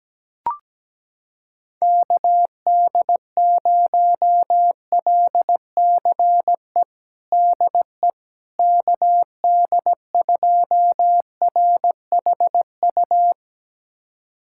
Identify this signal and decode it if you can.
Morse code, heard as a keyed tone, about 17 wpm, 700 Hz. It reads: KD0LCE DE KD2RHU
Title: USA Callsign DE Callsign - 6 Characters - 17wpm